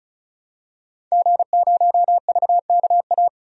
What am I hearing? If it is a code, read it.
Morse code, heard as a keyed tone, about 34 wpm, 700 Hz. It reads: G0VKA
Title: DXCC-100 Call Signs - 5 Characters - 35wpm